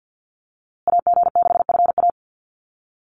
Morse code, heard as a keyed tone, about 49 wpm, 700 Hz. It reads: AG6FA
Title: USA Call Signs Repeated 3x 50wpm